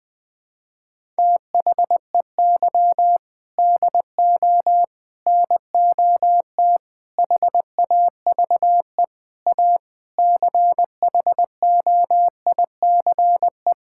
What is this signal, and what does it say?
Morse code, heard as a keyed tone, about 21 wpm, 700 Hz. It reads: THEY DO NOT HAVE A CHOICE
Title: Sentences with 5-Word Phrases - Spoken Beginning - 20wpm